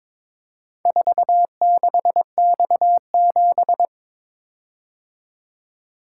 Morse code, heard as a keyed tone, about 23 wpm, 690 Hz. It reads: 46X7
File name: practice / 22wpm / Four Letter-Number Combinations - 22wpm